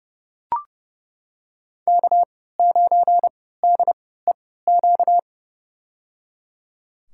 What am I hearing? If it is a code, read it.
Morse code, heard as a keyed tone, about 30 wpm, spaced as for 17 wpm, 700 Hz. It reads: K9DEQ
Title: USA Call Signs - 5 Characters - 3x Character Spacing - 30wpm